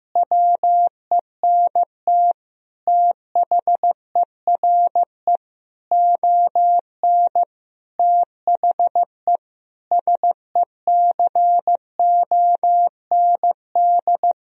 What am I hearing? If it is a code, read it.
Morse code, heard as a keyed tone, about 15 wpm, 700 Hz. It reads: WENT THERE ON THE SECOND
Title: Sentences from Top 1000 Words - Review Part 4 - 15wpm